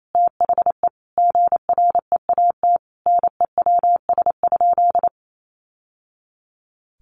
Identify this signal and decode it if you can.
Morse code, heard as a keyed tone, about 28 wpm, 700 Hz. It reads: THE GREAT NEWS?
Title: Sentences from Top 800 Words 28wpm